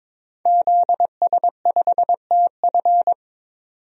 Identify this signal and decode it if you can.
Morse code, heard as a keyed tone, about 22 wpm, 700 Hz. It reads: ZS5TF
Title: DXCC-100 Call Signs - 5 Characters - 22wpm